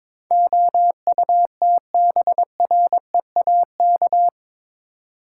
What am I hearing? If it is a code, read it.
Morse code, heard as a keyed tone, about 22 wpm, 700 Hz. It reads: OUTBREAK